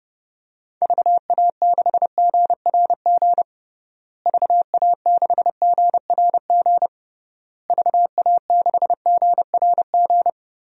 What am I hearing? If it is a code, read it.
Morse code, heard as a keyed tone, about 30 wpm, 700 Hz. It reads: VA6GRG VA6GRG VA6GRG
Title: DXCC-100 Call Signs - 6 Characters - Repeated 3x - 30wpm